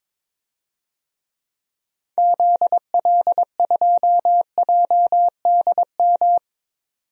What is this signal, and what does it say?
Morse code, heard as a keyed tone, about 22 wpm, 700 Hz. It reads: ZL2JDM
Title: DXCC-100 Call Signs - 6 Characters - 22wpm